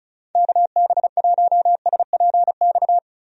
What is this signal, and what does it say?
Morse code, heard as a keyed tone, about 36 wpm, 700 Hz. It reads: KB1SPX
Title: USA Call Signs - 6 Characters - 35wpm